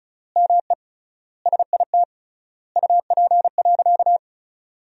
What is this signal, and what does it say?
Morse code, heard as a keyed tone, about 36 wpm, 700 Hz. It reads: ME SIT UP.
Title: 3x Word-Spacing - Sentences from Top 500 Words - 35wpm